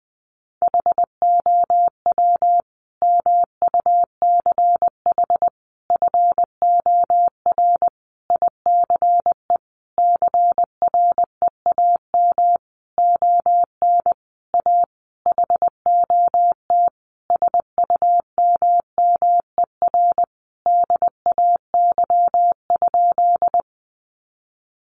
Morse code, heard as a keyed tone, about 20 wpm, 700 Hz. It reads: HOW MUCH FOR ICE CREAM ON A HOT SUMMER DAY?
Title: Sentences from Top 1500 Words 20wpm